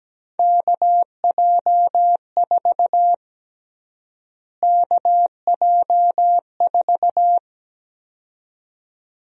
Morse code, heard as a keyed tone, about 17 wpm, 700 Hz. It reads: KJ4 KJ4
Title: USA Call Sign Prefixes - Repeated 2x - 17wpm